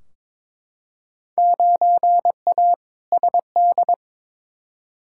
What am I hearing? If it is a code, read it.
Morse code, heard as a keyed tone, about 22 wpm, 700 Hz. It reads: 9A SD
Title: Field Day Exchanges Encore - 22wpm